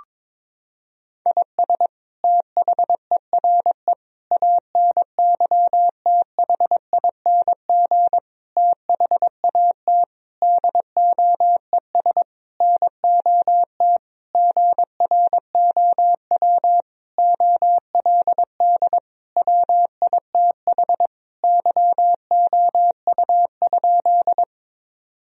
Morse code, heard as a keyed tone, about 22 wpm, 700 Hz. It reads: IS THERE ANYTHING THAT DOES NOT GROW OLD WITH YOU?
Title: Sentences from Top 1000 Words - Review Part 1 - 22wpm